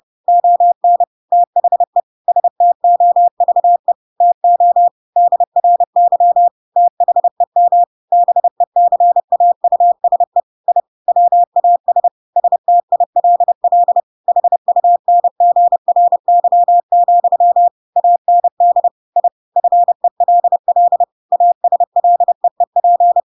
Morse code, heard as a keyed tone, about 30 wpm, 700 Hz. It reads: ON THE STOVE TO DRY THEM BECAUSE I WAS STILL HUNGRY, AND I FELL ASLEEP
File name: Pinocchio / Pinocchio ch7 30 30 ik1hge